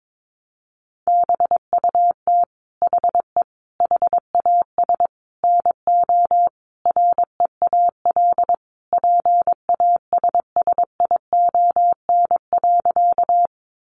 Morse code, heard as a keyed tone, about 22 wpm, 700 Hz. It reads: BUT HE HAS NO REAL PASSION.
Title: Sentences from Top 1300 Words 22wpm